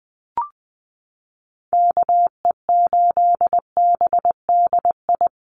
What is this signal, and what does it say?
Morse code, heard as a keyed tone, about 20 wpm, 700 Hz. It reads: KE8BDI
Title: USA Call Signs 20wpm